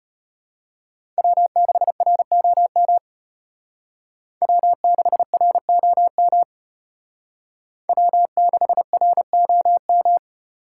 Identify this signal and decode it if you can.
Morse code, heard as a keyed tone, about 33 wpm, 700 Hz. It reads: W6ROM W6ROM W6ROM